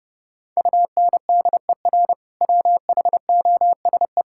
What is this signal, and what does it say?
Morse code, heard as a keyed tone, about 30 wpm, 700 Hz. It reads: UNDER WHOSE
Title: Sentences from Top 1600 Words 30wpm